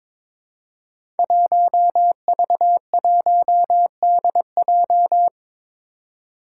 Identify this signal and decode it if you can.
Morse code, heard as a keyed tone, about 22 wpm, 700 Hz. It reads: 1V1DJ
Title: Five Challenging Letter-Number Combinations 22wpm